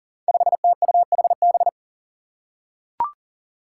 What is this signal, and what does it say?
Morse code, heard as a keyed tone, about 40 wpm, 700 Hz. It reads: 5TUHB
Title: Five Challenging Letter-Number Combinations 40wpm